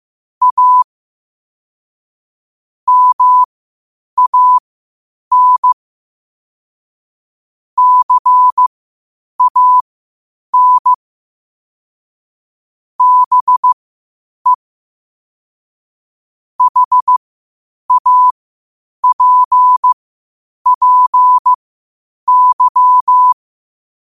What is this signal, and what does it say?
Morse code, heard as a keyed tone, about 15 wpm, spaced as for 8 wpm, 1000 Hz. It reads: A MAN CAN BE HAPPY
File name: qotdfiles / 8wpm